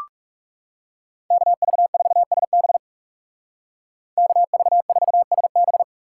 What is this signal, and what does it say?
Morse code, heard as a keyed tone, about 35 wpm, 700 Hz. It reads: KV4SB KV4SB
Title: Speed Racing - USA 2x2 Call Signs 30wpm